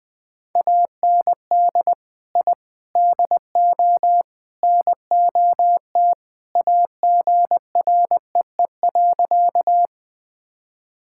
Morse code, heard as a keyed tone, about 20 wpm, 700 Hz. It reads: AND I DO NOT AGREE.
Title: Sentences with 4-Word Phrases - 20wpm